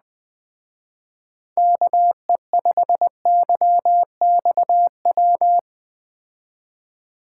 Morse code, heard as a keyed tone, about 20 wpm, 700 Hz. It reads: KE5YXW
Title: Speed Racing - USA 2x3 Call Signs 17wpm